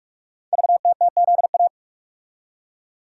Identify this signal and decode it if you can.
Morse code, heard as a keyed tone, about 44 wpm, 690 Hz. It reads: VTTZA
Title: Code Groups - 5 Letters - 45wpm